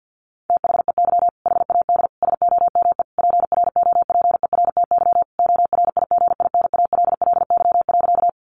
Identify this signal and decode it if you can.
Morse code, heard as a keyed tone, about 48 wpm, 700 Hz. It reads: THEY HAD SOME PROPERTY ORIGINALLY.
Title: Sentences from Top 1500 Words 50wpm